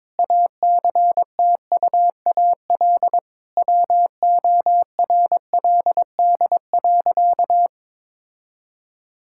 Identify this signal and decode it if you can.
Morse code, heard as a keyed tone, about 23 wpm, 700 Hz. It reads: ACTUAL WORLD.
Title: Sentences from Top 1000 Words - Review Part 5 - 22wpm